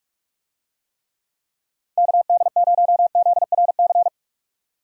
Morse code, heard as a keyed tone, about 46 wpm, 700 Hz. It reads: KD0ZRC
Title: USA Call Signs - 6 Characters - 45wpm